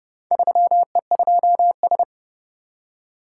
Morse code, heard as a keyed tone, about 30 wpm, 700 Hz. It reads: 3E2S